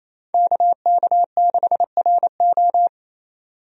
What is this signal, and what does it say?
Morse code, heard as a keyed tone, about 28 wpm, 700 Hz. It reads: KK6RO
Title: USA Call Signs - 5 Characters - 28wpm